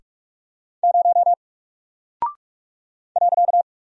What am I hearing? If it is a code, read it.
Morse code, heard as a keyed tone, about 48 wpm, 700 Hz. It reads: 0 .
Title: Single Character (Letter:Number:Punctuation:Prosign) Alt 45wpm